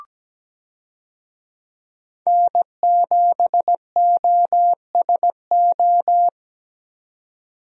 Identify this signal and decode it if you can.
Morse code, heard as a keyed tone, about 17 wpm, 700 Hz. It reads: N7OSO
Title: USA Call Signs - 5 Characters - 17wpm